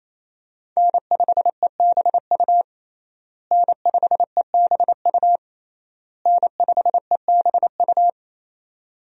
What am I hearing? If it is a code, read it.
Morse code, heard as a keyed tone, about 27 wpm, 700 Hz. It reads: N5EBU N5EBU N5EBU
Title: USA Call Signs - 5 Characters - Repeated 3x - 28wpm